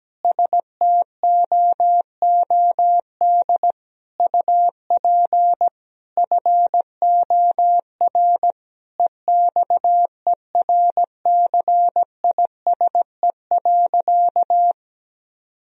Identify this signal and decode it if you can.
Morse code, heard as a keyed tone, about 17 wpm, 700 Hz. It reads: STOOD UP FOR EXERCISE.
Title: Sentences from Top 1000 Words - Review Part 4 - 17wpm